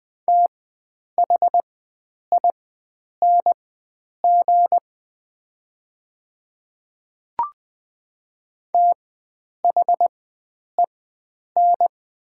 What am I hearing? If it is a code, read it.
Morse code, heard as a keyed tone, about 20 wpm, spaced as for 9 wpm, 700 Hz. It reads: THING THEN
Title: 4x Character Spacing - Top 100 Words - 20wpm